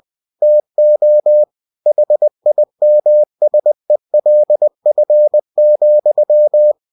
Morse code, heard as a keyed tone, about 20 wpm, 600 Hz. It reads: TO HIMSELF,
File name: cw-books / winnie-the-pooh / winnie-20wpm-page-0014